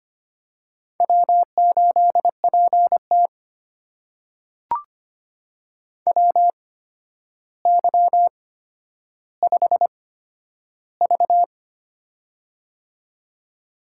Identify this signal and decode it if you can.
Morse code, heard as a keyed tone, about 25 wpm, spaced as for 11 wpm, 700 Hz. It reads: W8PT WY5V
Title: USA Call Signs - 4 Characters - 8x Character Spacing - 25wpm